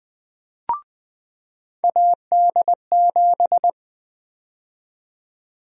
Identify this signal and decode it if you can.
Morse code, heard as a keyed tone, about 20 wpm, 700 Hz. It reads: AD7
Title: USA Call Sign Prefixes - 20wpm